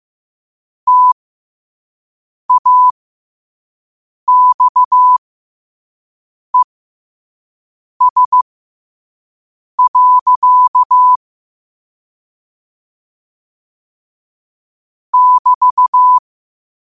Morse code, heard as a keyed tone, about 15 wpm, spaced as for 5 wpm, 1000 Hz. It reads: TAXES. <BT>
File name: qotdfiles / 5wpm